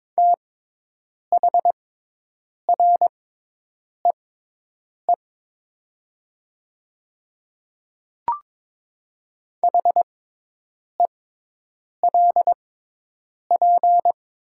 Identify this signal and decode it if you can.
Morse code, heard as a keyed tone, about 22 wpm, spaced as for 8 wpm, 700 Hz. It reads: THREE HELP
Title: 6x Character Spacing - Top 200 Words - 22wpm